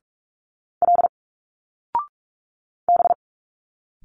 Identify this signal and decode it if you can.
Morse code, heard as a keyed tone, about 46 wpm, 700 Hz. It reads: L B
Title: Single Letters 45wpm